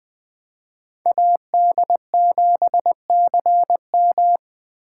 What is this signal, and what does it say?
Morse code, heard as a keyed tone, about 23 wpm, 700 Hz. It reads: AD7CM
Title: Speed Racing - USA 2x2 Call Signs 20wpm